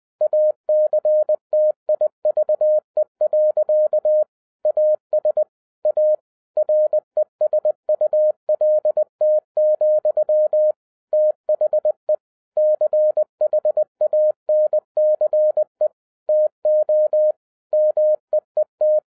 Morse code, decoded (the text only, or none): ACTIVE. AS A RESULT, THE CHANCE TO MEET